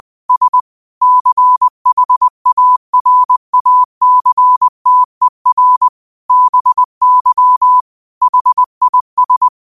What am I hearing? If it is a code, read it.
Morse code, heard as a keyed tone, about 20 wpm, 1000 Hz. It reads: S CHARACTER BY HIS